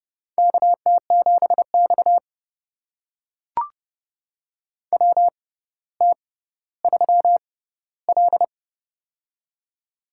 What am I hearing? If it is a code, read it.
Morse code, heard as a keyed tone, about 30 wpm, spaced as for 15 wpm, 700 Hz. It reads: KT7X WT3L
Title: USA Call Signs - 4 Characters - 6x Character Spacing - 30wpm